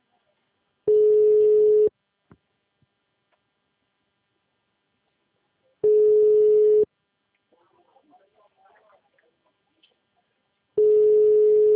Ringback tone (Europe, 425 Hz).